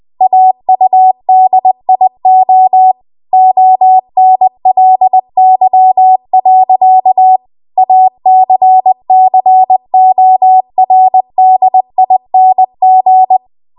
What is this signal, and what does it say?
Morse code, heard as a keyed tone, about 20 wpm, 750 Hz. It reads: AUDIO ONLY. ACCORDING
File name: Archive / 20 WPM / 20WPM